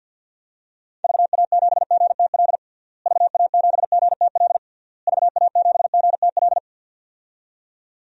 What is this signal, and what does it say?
Morse code, heard as a keyed tone, about 53 wpm, 700 Hz. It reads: VA7GTL VA7GTL VA7GTL
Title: DXCC-100 Call Signs - 6 Characters - Repeated 3x - 50wpm